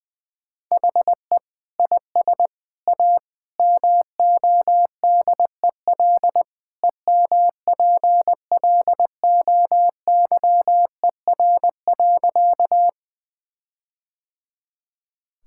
Morse code, heard as a keyed tone, about 20 wpm, 700 Hz. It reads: HE IS A MODEL EMPLOYER.
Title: Sentences from Top 1600 Words 20wpm